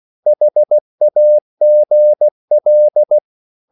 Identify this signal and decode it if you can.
Morse code, heard as a keyed tone, about 16 wpm, 600 Hz. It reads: HAGL
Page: Korte danske ord 16wpm